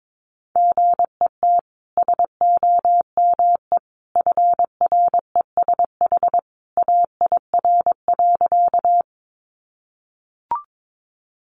Morse code, heard as a keyed tone, about 22 wpm, 700 Hz. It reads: GET SOME FRESH AIR.
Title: Sentences from Top 1200 Words 22wpm